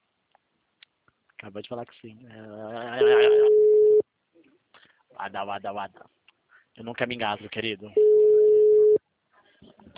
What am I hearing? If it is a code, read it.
Ringback tone (Europe, 425 Hz), with speech.
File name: out- -2007- - - .41122